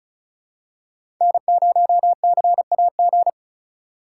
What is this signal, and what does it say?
Morse code, heard as a keyed tone, about 36 wpm, 700 Hz. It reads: N0CAG
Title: USA Call Signs - 5 Characters - 4x Character Spacing - 35wpm